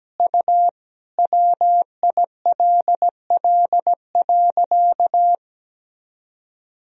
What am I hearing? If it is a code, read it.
Morse code, heard as a keyed tone, about 17 wpm, 700 Hz. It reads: U WILL.
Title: Sentences from Top 500 Words and Abbreviations 17wpm